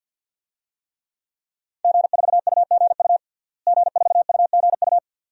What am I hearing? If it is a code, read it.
Morse code, heard as a keyed tone, about 53 wpm, 700 Hz. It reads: G4UGU G4UGU
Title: DXCC-100 Call Signs - 5 Characters - Repeated 2x - 50wpm